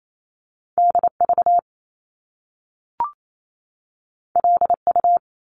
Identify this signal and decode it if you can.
Morse code, heard as a keyed tone, about 28 wpm, 700 Hz. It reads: DV LU
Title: Top Two Letter Combinations in Words 28wpm